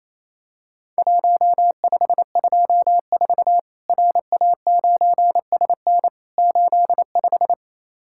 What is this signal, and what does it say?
Morse code, heard as a keyed tone, about 27 wpm, 700 Hz. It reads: 1524 RA9SN 85
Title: Worked All Europe DX Contest - QTC Exchange - 28wpm